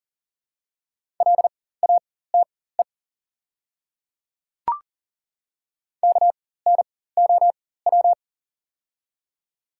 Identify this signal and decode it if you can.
Morse code, heard as a keyed tone, about 40 wpm, spaced as for 19 wpm, 700 Hz. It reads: LATE KNOW
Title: 4x Character Spacing - Top 300 Words - 40wpm